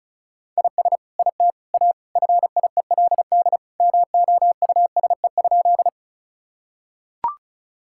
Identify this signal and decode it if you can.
Morse code, heard as a keyed tone, about 36 wpm, 690 Hz. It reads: IS IT A FIELD MOUSE?
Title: Sentences from Top 1000 Words - Review Part 5 - 35wpm